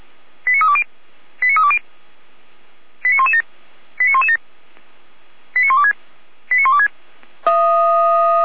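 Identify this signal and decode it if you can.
Five-tone selective call (ZVEI), heard as quick runs of five tones, tones of about 70 ms. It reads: 89320 89320 89117 89117 89126 89126
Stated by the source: ZVEI five-tone selective call 89045 received by SDR.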